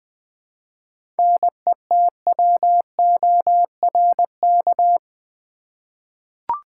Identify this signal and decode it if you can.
Morse code, heard as a keyed tone, about 20 wpm, 700 Hz. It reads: NETWORK
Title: Seven Letter Words - 20wpm